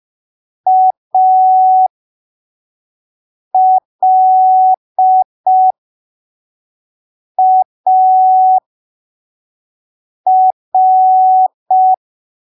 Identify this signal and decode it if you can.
Morse code, heard as a keyed tone, about 5 wpm, spaced as for 3 wpm, 750 Hz. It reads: ALAR
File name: arquivos / texto / texto 24 05ppm